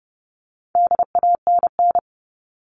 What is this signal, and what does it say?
Morse code, heard as a keyed tone, about 30 wpm, 700 Hz. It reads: DANN
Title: All Cut Numbers - 4-Digit 30wpm